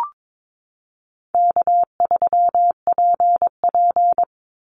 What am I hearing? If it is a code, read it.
Morse code, heard as a keyed tone, about 22 wpm, 700 Hz. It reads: K3PP
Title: Easy Call Signs - 22wpm